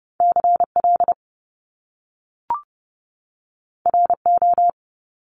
Morse code, heard as a keyed tone, about 30 wpm, 700 Hz. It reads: CL RO
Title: Top Two Letter Combinations in Words 30wpm